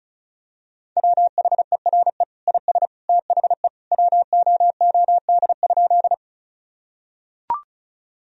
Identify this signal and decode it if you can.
Morse code, heard as a keyed tone, about 34 wpm, 700 Hz. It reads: WHERE IS THE WOOD?
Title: Sentences from Top 1000 Words - Review Part 2 - 35wpm